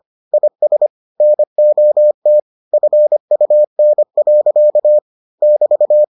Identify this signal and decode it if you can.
Morse code, heard as a keyed tone, about 25 wpm, 600 Hz. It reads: IS NOT FUN. <BT>